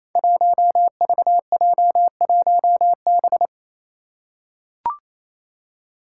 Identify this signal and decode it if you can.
Morse code, heard as a keyed tone, about 28 wpm, 700 Hz. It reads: 1VJ1B
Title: Five Challenging Letter-Number Combinations 28wpm